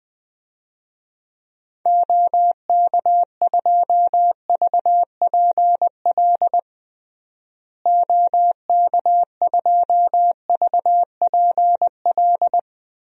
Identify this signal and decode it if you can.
Morse code, heard as a keyed tone, about 21 wpm, 700 Hz. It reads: OK2VPL OK2VPL
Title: DXCC-100 Call Signs - 6 Characters - Repeated 2x - 20wpm